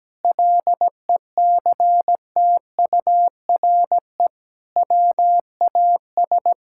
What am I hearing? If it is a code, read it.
Morse code, heard as a keyed tone, about 17 wpm, 700 Hz. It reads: LECTURE WAS
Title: Sentences from Top 1600 Words 17wpm